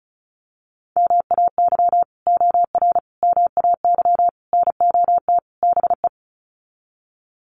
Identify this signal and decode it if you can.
Morse code, heard as a keyed tone, about 34 wpm, 700 Hz. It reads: MAY OR MAY NOT BE
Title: Sets of 5 Words 35wpm